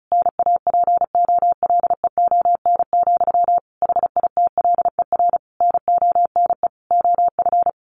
Morse code, heard as a keyed tone, about 34 wpm, 700 Hz. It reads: NAPOLEON, HITLER NONE OF
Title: A Sound Of Thunder 35wpm